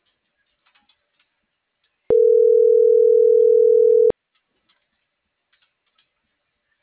Ringback tone (North America), 440 + 480 Hz.